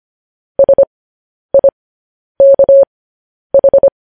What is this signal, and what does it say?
Morse code, heard as a keyed tone, about 25 wpm, spaced as for 10 wpm, 550 Hz. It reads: SIKH
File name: podcast / News headlines-1x-25x10